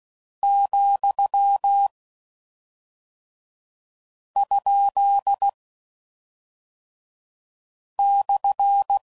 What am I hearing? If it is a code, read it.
Morse code, heard as a keyed tone, about 16 wpm, 800 Hz. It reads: , ? /